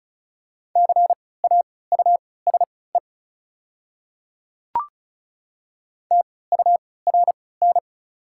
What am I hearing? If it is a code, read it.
Morse code, heard as a keyed tone, about 36 wpm, spaced as for 20 wpm, 700 Hz. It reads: CAUSE TURN
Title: 3x Character Spacing - Top 300 Words - 35wpm